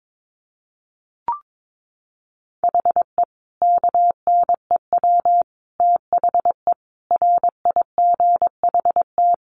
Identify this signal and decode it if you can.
Morse code, heard as a keyed tone, about 22 wpm, 700 Hz. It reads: HE KNEW THE RIGHT
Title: Sentences from Top 1300 Words 22wpm